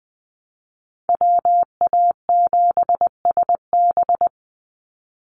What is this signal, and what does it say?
Morse code, heard as a keyed tone, about 20 wpm, 690 Hz. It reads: WA7SB
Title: NA SOTA Chaser Callsigns 20wpm